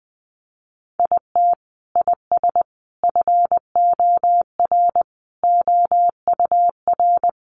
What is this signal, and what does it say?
Morse code, heard as a keyed tone, about 20 wpm, 700 Hz. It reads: IT IS FOR OUR
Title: Sentences from Top 700 Words 20wpm